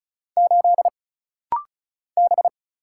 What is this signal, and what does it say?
Morse code, heard as a keyed tone, about 34 wpm, 700 Hz. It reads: 8 B
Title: Single Letter-Number Alt - Rapid-Fire 35wpm